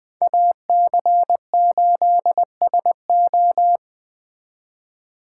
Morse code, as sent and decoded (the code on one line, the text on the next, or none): .- -.-. ---.. ... ---
AC8SO